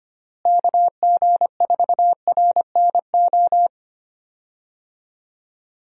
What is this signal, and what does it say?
Morse code, heard as a keyed tone, about 25 wpm, 700 Hz. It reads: KG4RNO
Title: USA Call Signs - 6 Characters - Repeated 3x - 25wpm